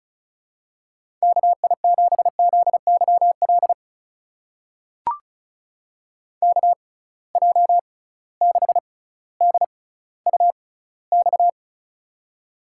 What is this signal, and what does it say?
Morse code, heard as a keyed tone, about 36 wpm, spaced as for 18 wpm, 700 Hz. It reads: KI7ZYL KJ6DUX
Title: USA Call Signs - 6x Character Spacing - 35wpm